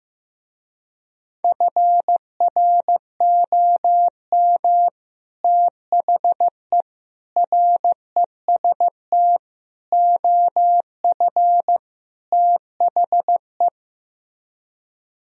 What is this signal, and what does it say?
Morse code, heard as a keyed tone, about 16 wpm, 700 Hz. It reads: FROM THE REST OF THE
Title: Speed-Racing - Sets of 5 Words 15wpm